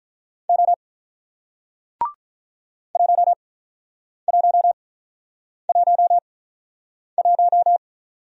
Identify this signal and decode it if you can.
Morse code, heard as a keyed tone, about 47 wpm, spaced as for 9 wpm, 700 Hz. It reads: O 1111